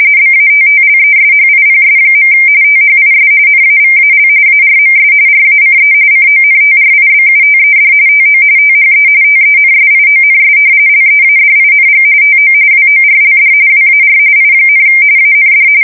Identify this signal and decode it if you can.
RTTY, about 45.45 baud. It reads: BATTLESHIPS. THIS NAVAL ARMS RACE CONTRIBUTED TO THE TENSIONS LEADING TO WORLD WAR I, AND
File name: fsk170 / battleshipnj